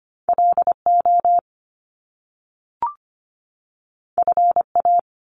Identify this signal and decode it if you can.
Morse code, heard as a keyed tone, about 25 wpm, 700 Hz. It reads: LO FA
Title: Top Two Letter Combinations in Words 25wpm